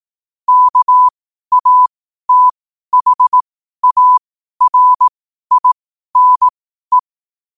Morse code, heard as a keyed tone, about 18 wpm, spaced as for 12 wpm, 1000 Hz. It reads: KATHARINE
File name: qotdfiles / 12wpm